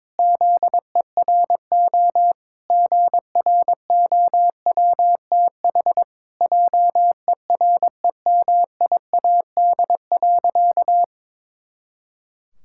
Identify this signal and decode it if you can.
Morse code, heard as a keyed tone, about 22 wpm, 700 Hz. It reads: ZERO GROWTH JEREMIAD.
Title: Pangram Challenge v2 - 22wpm